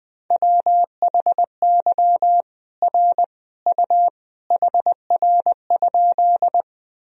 Morse code, heard as a keyed tone, about 21 wpm, 700 Hz. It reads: WHY R U HR?
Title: Sentences from Top 200 Words and Abbreviations 20wpm